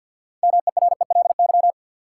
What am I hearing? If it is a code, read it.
Morse code, heard as a keyed tone, about 52 wpm, 700 Hz. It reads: MERELY
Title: Sentences from Top 1400 Words - New Word Drill - 50wpm